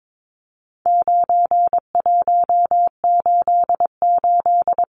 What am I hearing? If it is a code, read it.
Morse code, heard as a keyed tone, about 22 wpm, 700 Hz. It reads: 9188
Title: Numbers - 4-Digit 22wpm